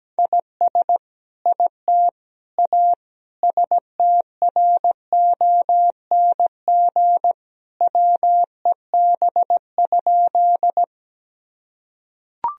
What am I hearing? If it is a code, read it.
Morse code, heard as a keyed tone, about 17 wpm, 700 Hz. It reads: IS IT A STRONG WEB?
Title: Sentences from Top 1000 Words - Review Part 4 - 17wpm